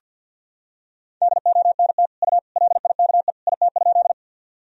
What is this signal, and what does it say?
Morse code, heard as a keyed tone, about 46 wpm, 700 Hz. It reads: DONT U LIKE IT?
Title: Sentences from Top 300 Words and Abbreviations 50wpm